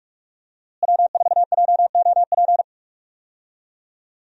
Morse code, heard as a keyed tone, about 42 wpm, 700 Hz. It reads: W4JOP
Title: USA Call Signs - 5 Characters - Repeated 3x - 45wpm